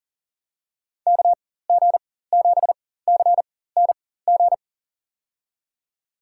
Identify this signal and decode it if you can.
Morse code, heard as a keyed tone, about 37 wpm, spaced as for 18 wpm, 700 Hz. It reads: KG7CNG
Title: USA Call Signs - 6 Characters - 4x Character Spacing - 40wpm